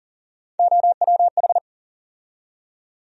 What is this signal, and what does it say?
Morse code, heard as a keyed tone, about 40 wpm, 700 Hz. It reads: OWH
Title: USA Call Sign Postfixes - 3 Letters 40wpm